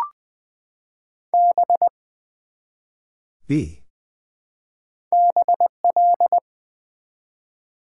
Morse code, heard as a keyed tone, about 20 wpm, 700 Hz. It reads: B BL
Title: Build A Word - 1 Letter At A Time - 5-Letter Words - 20wpm